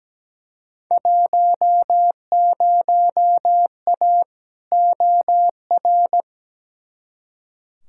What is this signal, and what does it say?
Morse code, heard as a keyed tone, about 17 wpm, 700 Hz. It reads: 10A OR
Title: Field Day Exchanges - 17wpm